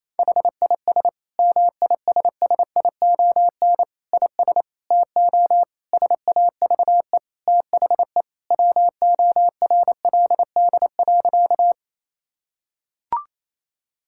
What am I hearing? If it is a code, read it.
Morse code, heard as a keyed tone, about 28 wpm, 700 Hz. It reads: HIS MISSION IS TO SAVE THE WORLD.
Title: Sentences from Top 1600 Words 28wpm